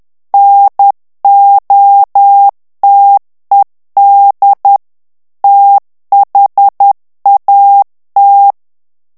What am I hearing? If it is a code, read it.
Morse code, heard as a keyed tone, about 10 wpm, 800 Hz. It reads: NOTED THAT
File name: ANGLAIS texte clair 3 vitesse 10 mots minute